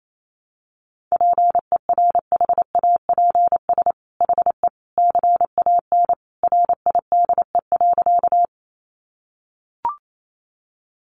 Morse code, heard as a keyed tone, about 28 wpm, 700 Hz. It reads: PERHAPS HE CAN RIDE.
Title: Sentences from Top 1200 Words 28wpm